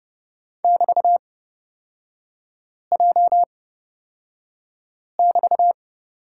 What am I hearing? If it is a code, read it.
Morse code, heard as a keyed tone, about 30 wpm, spaced as for 5 wpm, 700 Hz. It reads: <BT>J<BT>
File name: Lesson 36e - BT - Three Characters 30wpm